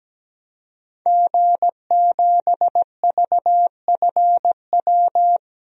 Morse code, heard as a keyed tone, about 17 wpm, 700 Hz. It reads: G7VFW
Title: DXCC-100 Call Signs - 5 Characters - Repeated 3x - 17wpm